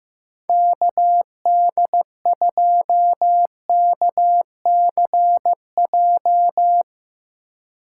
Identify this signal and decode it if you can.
Morse code, heard as a keyed tone, about 15 wpm, 700 Hz. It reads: KD2KCJ
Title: USA Call Signs - 6 Characters - 4x Character Spacing - 15wpm